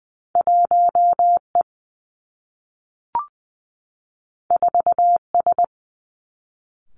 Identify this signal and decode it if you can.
Morse code, heard as a keyed tone, about 20 wpm, 700 Hz. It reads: 1 E 4 S
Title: Two Challenging Letter-Number Combinations 20wpm